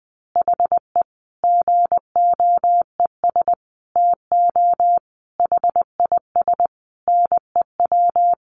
Morse code, heard as a keyed tone, about 20 wpm, 700 Hz. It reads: HE GOES TO HIS NEW
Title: Easy Phrases 20wpm